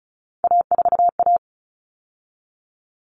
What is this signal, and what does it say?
Morse code, heard as a keyed tone, about 34 wpm, 700 Hz. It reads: A4A
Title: All Cut Numbers - 3-Digit 35wpm